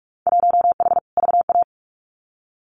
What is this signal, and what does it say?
Morse code, heard as a keyed tone, about 46 wpm, 700 Hz. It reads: 1H VA
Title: Winter Field Day Exchanges 45wpm